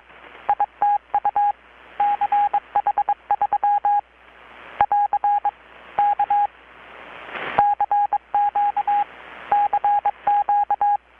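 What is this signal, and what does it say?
Morse code, heard as a keyed tone, about 22 wpm, 820 Hz. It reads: ITU CH3 <AR> K CQ CQ
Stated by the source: Morse from KPH's Night of Nights XX maritime radiotelegraphy event, received in Chiapas, Mexico about 3,800 km away on a Kenwood R-600 with a grounded 10 m long-wire antenna.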